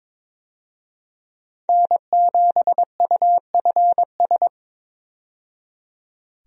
Morse code, heard as a keyed tone, about 22 wpm, 700 Hz. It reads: N7UFS
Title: USA Call Signs - 5 Characters - 22wpm